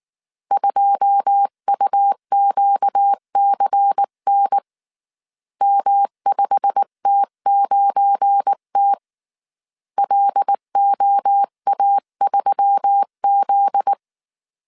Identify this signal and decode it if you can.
Morse code, heard as a keyed tone, about 19 wpm, 800 Hz. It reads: 2UQCN M5T9T LOA3Z
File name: CW / DET / 1 9 1500C S C 18wpm 2539